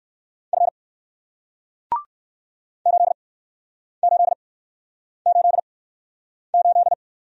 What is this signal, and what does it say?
Morse code, heard as a keyed tone, about 60 wpm, spaced as for 9 wpm, 700 Hz. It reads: F 8888